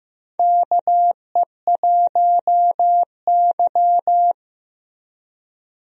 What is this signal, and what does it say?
Morse code, heard as a keyed tone, about 15 wpm, 700 Hz. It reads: KE1Y
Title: USA Call Signs - 4 Characters - Repeated 2x - 15wpm